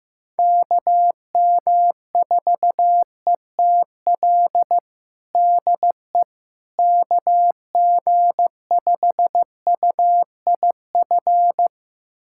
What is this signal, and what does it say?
Morse code, heard as a keyed tone, about 15 wpm, 700 Hz. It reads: KM4ETL DE KG5UIF
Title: USA Callsign DE Callsign - 6 Characters - 15wpm